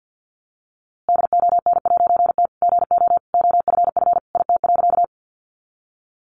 Morse code, heard as a keyed tone, about 49 wpm, 700 Hz. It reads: DON'T GO OFF IT.
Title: A Sound Of Thunder 50wpm